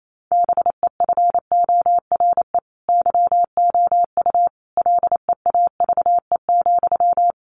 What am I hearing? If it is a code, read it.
Morse code, heard as a keyed tone, about 28 wpm, 700 Hz. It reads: BEFORE YOU LEAVE,
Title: Sentences from Top 400 Words 28wpm